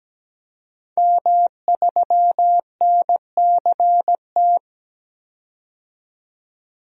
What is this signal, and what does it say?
Morse code, heard as a keyed tone, about 17 wpm, 700 Hz. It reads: M3NCT
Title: DXCC-100 Call Signs - 5 Characters - 17wpm